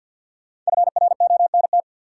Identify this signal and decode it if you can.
Morse code, heard as a keyed tone, about 47 wpm, 700 Hz. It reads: FRONT
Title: Sentences from Top 500 Words - New Word Drill - 50wpm